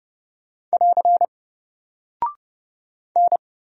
Morse code, heard as a keyed tone, about 30 wpm, 700 Hz. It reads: <AR> N
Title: Single Character (Letter:Number:Punctuation:Prosign) Alt 30wpm